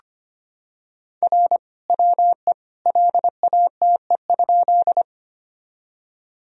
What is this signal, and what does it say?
Morse code, heard as a keyed tone, about 25 wpm, 700 Hz. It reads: R WE LATE?
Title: Sentences from Top 300 Words and Abbreviations 25wpm